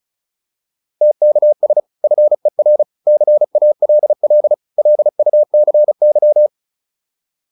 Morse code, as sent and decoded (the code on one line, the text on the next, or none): - -.- ... / ..-. .-. / -.-. .- .-.. .-.. / .-.. ..- -.-. -.--
TKS FER CALL LUCY